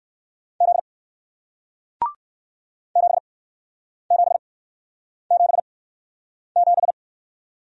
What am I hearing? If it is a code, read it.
Morse code, heard as a keyed tone, about 60 wpm, spaced as for 9 wpm, 700 Hz. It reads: Z 7777